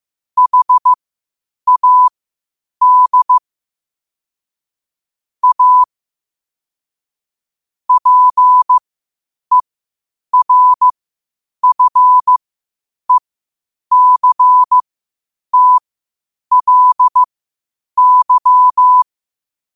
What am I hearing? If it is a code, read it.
Morse code, heard as a keyed tone, about 15 wpm, spaced as for 8 wpm, 1000 Hz. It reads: HAD A PERFECTLY
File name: qotdfiles / 8wpm